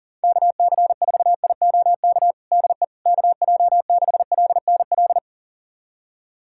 Morse code, heard as a keyed tone, about 40 wpm, 700 Hz. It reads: KC4IOK DE KJ6LNL
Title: USA Callsign DE Callsign - 6 Characters - 40wpm